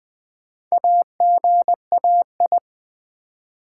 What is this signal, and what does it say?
Morse code, heard as a keyed tone, about 20 wpm, 700 Hz. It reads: AGAI
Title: Build A Word - 1 Letter At A Time - 5-Letter Words - 20wpm